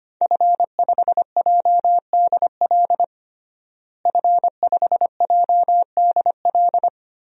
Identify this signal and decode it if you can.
Morse code, heard as a keyed tone, about 25 wpm, 700 Hz. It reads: F5JDL F5JDL
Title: DXCC-100 Call Signs - 5 Characters - Repeated 2x - 25wpm